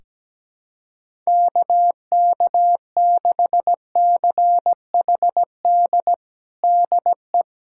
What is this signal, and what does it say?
Morse code, heard as a keyed tone, about 17 wpm, 700 Hz. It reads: KK6CHD DE
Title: USA Callsign DE Callsign - 6 Characters - 17wpm